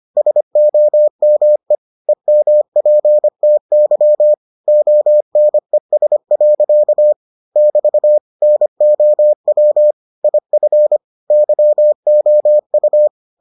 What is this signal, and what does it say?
Morse code, heard as a keyed tone, about 25 wpm, 600 Hz. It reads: SOME EMPTY ONES. <BT> NOW IF YOU